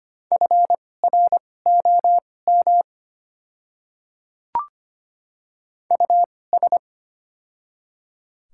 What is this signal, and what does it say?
Morse code, heard as a keyed tone, about 25 wpm, spaced as for 18 wpm, 690 Hz. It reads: FROM US
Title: 2x Character Spacing - Top 300 Words - 25wpm